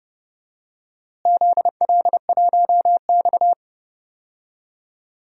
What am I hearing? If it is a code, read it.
Morse code, heard as a keyed tone, about 30 wpm, 700 Hz. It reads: ZL1X
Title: DXCC-100 Call Signs - 4 Characters - Repeated 2x - 30wpm